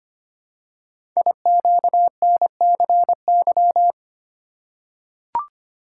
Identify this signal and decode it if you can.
Morse code, heard as a keyed tone, about 25 wpm, 700 Hz. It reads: IQNCY
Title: Code Groups - 5 Letters - 25wpm